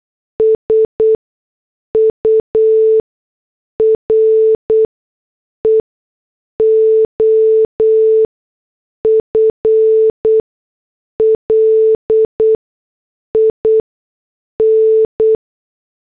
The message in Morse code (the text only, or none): SURE OF LIN